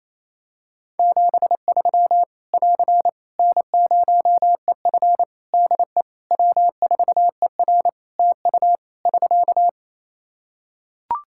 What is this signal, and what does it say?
Morse code, heard as a keyed tone, about 28 wpm, 700 Hz. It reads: 73 <AR> N0EF DE W4ER TU <SK>
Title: Practice QSO - Exchange 4 - Ending - 28wpm